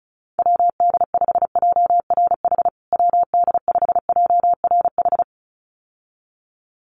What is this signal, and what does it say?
Morse code, heard as a keyed tone, about 34 wpm, 700 Hz. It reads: WD5JRH WD5JRH
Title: USA Call Signs Repeated 2x 35wpm